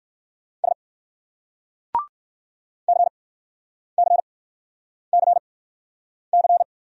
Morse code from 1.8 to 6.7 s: CCCC

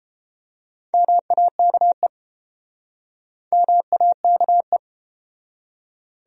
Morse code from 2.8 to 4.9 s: MAKE

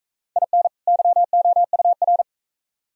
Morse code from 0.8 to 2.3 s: YOUR

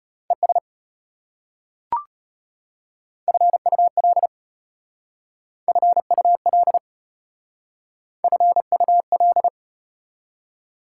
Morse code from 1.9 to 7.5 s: FUL FUL

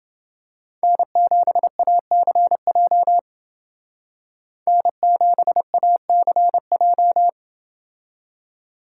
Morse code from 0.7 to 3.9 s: N7ACJ

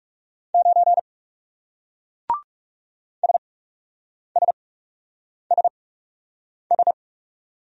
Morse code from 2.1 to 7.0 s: SSSS

